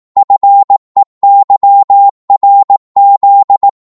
Morse code, heard as a keyed tone, about 18 wpm, 800 Hz. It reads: FEYRZ
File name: Morse / ltr-18wpm